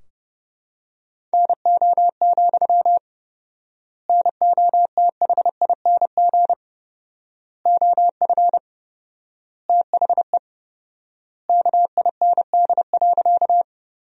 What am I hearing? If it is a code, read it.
Morse code, heard as a keyed tone, about 32 wpm, 700 Hz. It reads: NO, NOTHING OF THE KIND.
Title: 4x Word-Spacing - Sentences from Top 500 Words - 30wpm